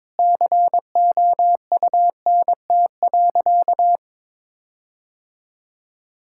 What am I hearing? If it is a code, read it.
Morse code, heard as a keyed tone, about 22 wpm, 700 Hz. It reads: COUNT.
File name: Difficult Sentences with Hints - 22wpm